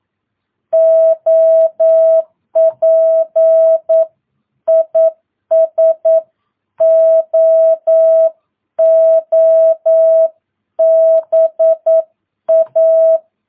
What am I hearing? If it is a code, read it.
Morse code, heard as a keyed tone, about 9 wpm, 660 Hz. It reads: OP IS OOBA
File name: jh8cbh / 2022cw